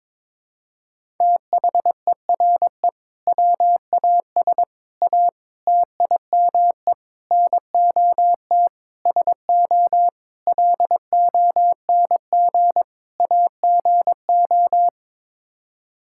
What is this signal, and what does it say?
Morse code, heard as a keyed tone, about 22 wpm, 700 Hz. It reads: THERE WAS A TIME NOT SO LONG AGO
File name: practice / 22wpm / Sets of 8 Words - 22wpm